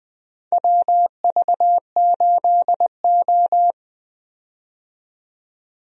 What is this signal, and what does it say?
Morse code, heard as a keyed tone, about 20 wpm, 700 Hz. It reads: WV8O
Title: USA Call Signs - 4 Characters - 20wpm